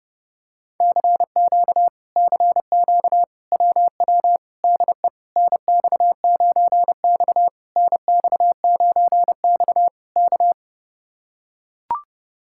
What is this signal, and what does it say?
Morse code, heard as a keyed tone, about 29 wpm, 700 Hz. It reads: CQ CQ WW DE NX9X NX9X K